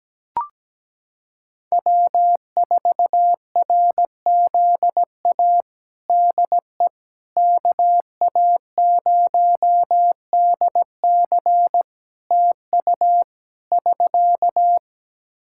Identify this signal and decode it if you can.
Morse code, heard as a keyed tone, about 17 wpm, 700 Hz. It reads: W4RZA DE KA0DC TU <SK>
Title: Practice QSO - Exchange 4 - Ending - 17wpm